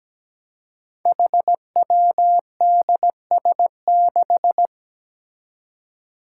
Morse code, heard as a keyed tone, about 17 wpm, 700 Hz. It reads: HWDS6
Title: Five Challenging Letter-Number Combinations 17wpm